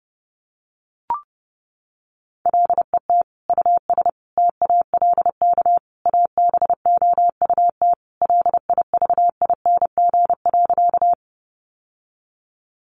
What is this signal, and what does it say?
Morse code, heard as a keyed tone, about 30 wpm, 700 Hz. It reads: LET US TALK ABOUT LIVING.
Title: Sentences from Top 1100 Words 30wpm